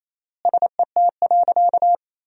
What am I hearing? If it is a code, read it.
Morse code, heard as a keyed tone, about 28 wpm, 700 Hz. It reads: SET.
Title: 3x Word-Spacing - Sentences from Top 400 Words - 28wpm